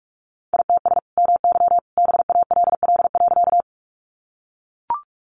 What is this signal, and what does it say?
Morse code, heard as a keyed tone, about 44 wpm, 700 Hz. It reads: ITS MY BALL.
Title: Sentences from Top 900 Words 45wpm